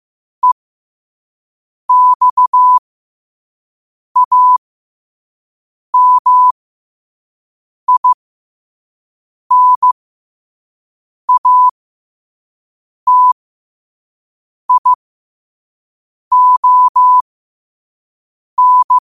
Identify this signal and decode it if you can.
Morse code, heard as a keyed tone, about 15 wpm, spaced as for 5 wpm, 1000 Hz. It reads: EXAMINATION
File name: qotdfiles / 5wpm